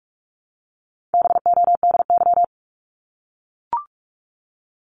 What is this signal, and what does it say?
Morse code, heard as a keyed tone, about 44 wpm, 700 Hz. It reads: BODY
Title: Top 500 Words Repeated 45wpm